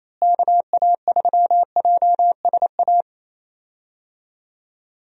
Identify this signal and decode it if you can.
Morse code, heard as a keyed tone, about 28 wpm, 700 Hz. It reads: KA3JSA
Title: USA Call Signs - 6 Characters - Repeated 3x - 28wpm